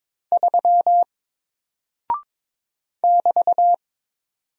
Morse code, heard as a keyed tone, about 22 wpm, 700 Hz. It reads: 3 <BT>
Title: Single Character (Letter:Number:Punctuation:Prosign) Alt 22wpm